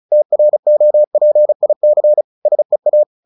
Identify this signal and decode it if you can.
Morse code, heard as a keyed tone, about 34 wpm, 600 Hz. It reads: TROPIC SEA
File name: the old man and the sea / the old man and the sea 1 35wpm